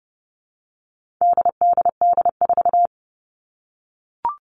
Morse code, heard as a keyed tone, about 30 wpm, 700 Hz. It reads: DDD4
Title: All Cut Numbers - 4-Digit 30wpm